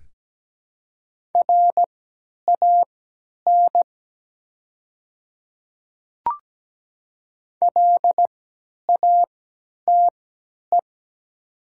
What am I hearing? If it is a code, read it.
Morse code, heard as a keyed tone, about 17 wpm, spaced as for 10 wpm, 700 Hz. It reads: RAN LATE
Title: 3x Character Spacing - Top 500 Words - 17wpm